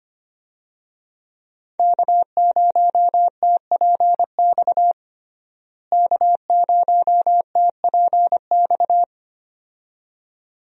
Morse code, heard as a keyed tone, about 25 wpm, 700 Hz. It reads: K0TPX K0TPX
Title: USA Call Signs - 5 Characters - Repeated 2x - 25wpm